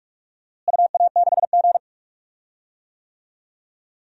Morse code, heard as a keyed tone, about 46 wpm, 700 Hz. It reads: UA6G
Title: DXCC-100 Call Signs - 4 Characters - Repeated 3x - 45wpm